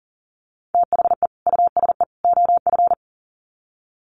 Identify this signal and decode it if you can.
Morse code, heard as a keyed tone, about 40 wpm, 700 Hz. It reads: THE USE OF
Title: Sets of 3 Words 40wpm